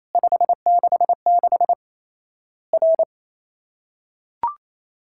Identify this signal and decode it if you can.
Morse code, heard as a keyed tone, about 28 wpm, 700 Hz. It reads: 566 R
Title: Worked All Europe DX Contest - QTC Exchange - 28wpm